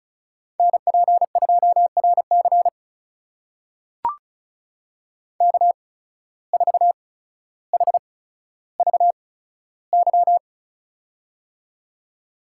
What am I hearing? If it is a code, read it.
Morse code, heard as a keyed tone, about 36 wpm, spaced as for 15 wpm, 700 Hz. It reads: NP2RC K4HVY
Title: USA Call Signs - 5 Characters - 8x Character Spacing - 35wpm